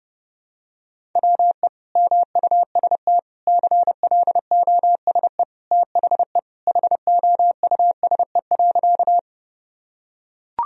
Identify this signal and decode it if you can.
Morse code, heard as a keyed tone, about 30 wpm, 700 Hz. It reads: WE MUST CLOSE THE HOUSE.
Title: Sentences from Top 1000 Words - Review Part 1 - 30wpm